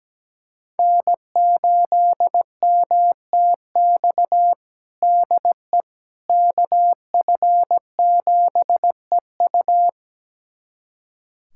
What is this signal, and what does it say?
Morse code, heard as a keyed tone, about 17 wpm, 700 Hz. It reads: N8MTX DE KF7EU
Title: USA Callsign DE Callsign - 5 Characters - 17wpm